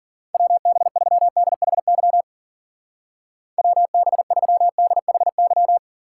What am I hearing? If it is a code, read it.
Morse code, heard as a keyed tone, about 49 wpm, 700 Hz. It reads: WB3DHY WB3DHY
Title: Speed Racing - USA Call Signs 40wpm